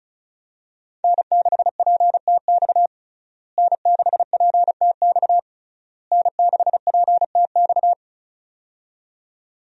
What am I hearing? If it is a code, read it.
Morse code, heard as a keyed tone, about 36 wpm, 700 Hz. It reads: N6PTX N6PTX N6PTX
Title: USA Call Signs - 5 Characters - Repeated 3x - 35wpm